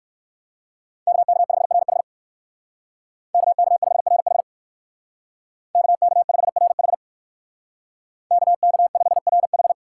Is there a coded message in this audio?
Yes: Morse code from 1.1 to 9.7 s, heard as a keyed tone, about 56 wpm, 700 Hz. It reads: KK5RH KK5RH KK5RH KK5RH